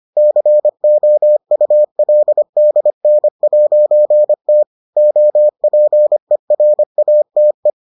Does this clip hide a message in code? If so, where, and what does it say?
Yes: Morse code from 0.2 to 7.7 s, heard as a keyed tone, about 25 wpm, 600 Hz. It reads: COULDN'T OPERATE